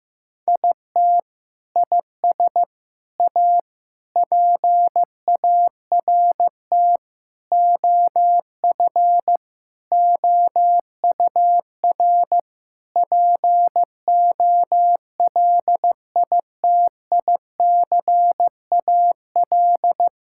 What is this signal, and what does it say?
Morse code, heard as a keyed tone, about 15 wpm, 700 Hz. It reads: IT IS A PART OF OUR POLITICAL
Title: Sentences from Top 1000 Words - Review Part 5 - 15wpm